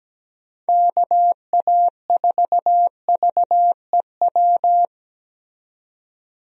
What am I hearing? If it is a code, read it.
Morse code, heard as a keyed tone, about 17 wpm, 700 Hz. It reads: KA4VEW
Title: USA Call Signs - 6 Characters - 17wpm